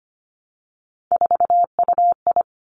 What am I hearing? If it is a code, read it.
Morse code, heard as a keyed tone, about 25 wpm, 700 Hz. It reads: 4UI
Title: Three Challenging Letter-Number Combinations 25wpm